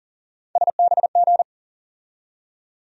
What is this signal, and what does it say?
Morse code, heard as a keyed tone, about 37 wpm, 700 Hz. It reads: SBG
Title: USA Call Sign Postfixes - 3 Letters 40wpm